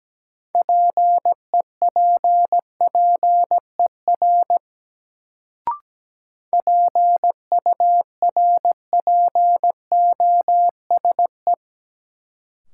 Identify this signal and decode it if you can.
Morse code, heard as a keyed tone, about 17 wpm, 700 Hz. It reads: PEPPER PURPOSE
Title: Sets of 5 Words - New Word Drill - 17wpm